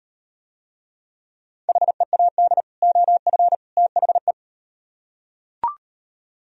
Morse code, heard as a keyed tone, about 38 wpm, 700 Hz. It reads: HEAD OF THE